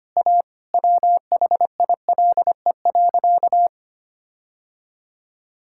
Morse code, heard as a keyed tone, about 25 wpm, 700 Hz. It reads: A WHILE.
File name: practice / 25wpm / Sentences with 4-Word Phrases - 25wpm